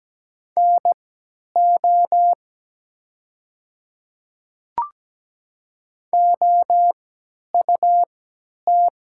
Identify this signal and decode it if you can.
Morse code, heard as a keyed tone, about 17 wpm, spaced as for 10 wpm, 700 Hz. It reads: NO OUT
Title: 3x Character Spacing - Top 200 Words - 17wpm